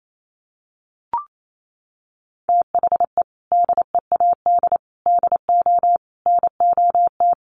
Morse code, heard as a keyed tone, about 28 wpm, 700 Hz. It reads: THE DEAD DO NOT
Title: Sentences from Top 1500 Words 28wpm